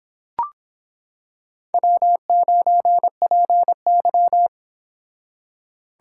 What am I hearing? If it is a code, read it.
Morse code, heard as a keyed tone, about 25 wpm, 700 Hz. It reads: W9PY